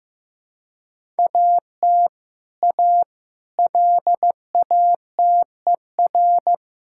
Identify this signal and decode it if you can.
Morse code, heard as a keyed tone, about 16 wpm, 700 Hz. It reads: AT A LATER
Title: Sentences from Top 1000 Words - Review Part 5 - 15wpm